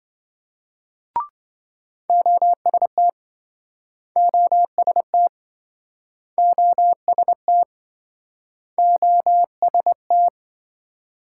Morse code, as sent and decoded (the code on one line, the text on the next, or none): --- ... - / --- ... - / --- ... - / --- ... -
OST OST OST OST